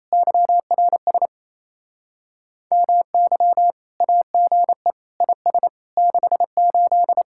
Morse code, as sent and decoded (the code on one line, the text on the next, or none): -.-- .-. ... / -- -.-- / .- --. / .. ... / -.... ---..
YRS MY AGE IS 68